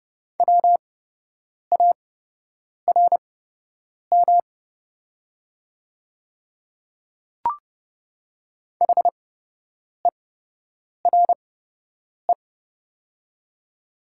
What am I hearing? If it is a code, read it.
Morse code, heard as a keyed tone, about 30 wpm, spaced as for 8 wpm, 700 Hz. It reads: WARM HERE
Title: 8x Character Spacing - Top 500 Words - 30wpm